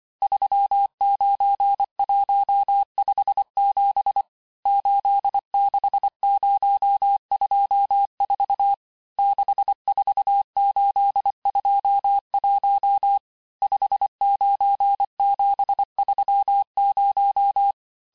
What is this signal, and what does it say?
Morse code, heard as a keyed tone, about 25 wpm, 800 Hz. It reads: 39157 86024 64821 59730
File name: texte-chiffre-vitesse-23mpm